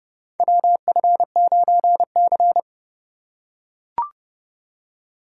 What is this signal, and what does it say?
Morse code, heard as a keyed tone, about 30 wpm, 700 Hz. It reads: WF9C
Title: USA Call Signs - 4 Characters - 8x Character Spacing - 30wpm